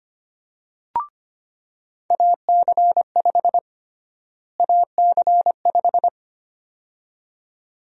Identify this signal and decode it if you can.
Morse code, heard as a keyed tone, about 25 wpm, 700 Hz. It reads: AC5 AC5
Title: USA Call Sign Prefixes - Repeated 2x - 25wpm